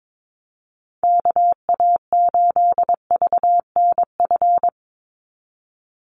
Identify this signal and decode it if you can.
Morse code, heard as a keyed tone, about 22 wpm, 700 Hz. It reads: KA8VNF
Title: USA Call Signs 22wpm